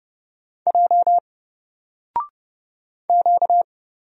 Morse code, heard as a keyed tone, about 30 wpm, 700 Hz. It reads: J Q
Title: Single Letter-Number Alt 30wpm